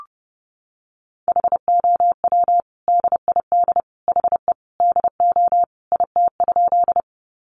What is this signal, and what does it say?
Morse code, heard as a keyed tone, about 28 wpm, 700 Hz. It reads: HOW DID HE DO IT?
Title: Sentences from Top 100 Words 30wpm